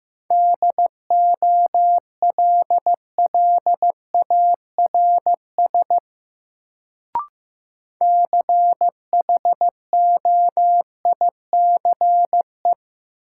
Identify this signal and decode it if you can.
Morse code, heard as a keyed tone, about 15 wpm, 700 Hz. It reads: DOLLARS CHOICE
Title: Sets of 5 Words - New Word Drill - 15wpm